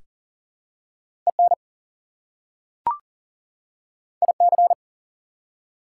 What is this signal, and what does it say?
Morse code, heard as a keyed tone, about 39 wpm, 700 Hz. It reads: EN IC
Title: Prefixes and Suffixes 40wpm